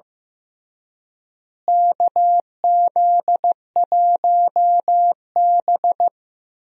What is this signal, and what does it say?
Morse code, heard as a keyed tone, about 16 wpm, 700 Hz. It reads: KZ1B
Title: USA Call Signs - 4 Characters - Repeated 3x - 15wpm